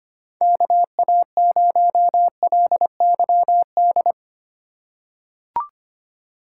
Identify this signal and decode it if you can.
Morse code, heard as a keyed tone, about 25 wpm, 700 Hz. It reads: KA0LYD
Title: USA Call Signs - 6 Characters - 4x Character Spacing - 25wpm